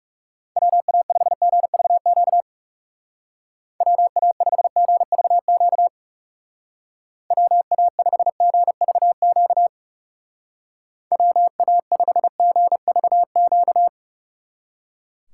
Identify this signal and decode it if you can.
Morse code, heard as a keyed tone, about 36 wpm, 700 Hz. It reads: WA5GVQ WA5GVQ WA5GVQ WA5GVQ
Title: Speed Racing - USA 2x3 Call Signs 30wpm